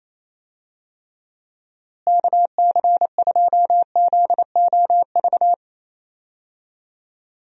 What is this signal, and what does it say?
Morse code, heard as a keyed tone, about 28 wpm, 700 Hz. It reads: KC2ZOV
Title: USA Call Signs - 6 Characters - Repeated 2x - 28wpm